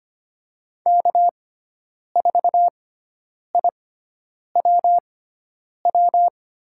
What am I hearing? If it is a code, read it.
Morse code, heard as a keyed tone, about 25 wpm, spaced as for 9 wpm, 700 Hz. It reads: K4IWW
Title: USA Call Signs - 5 Characters - 6x Character Spacing - 25wpm